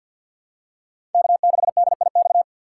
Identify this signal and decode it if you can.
Morse code, heard as a keyed tone, about 52 wpm, 700 Hz. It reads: K6DIX
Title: USA Call Signs - 5 Characters - 2x Character Spacing - 50wpm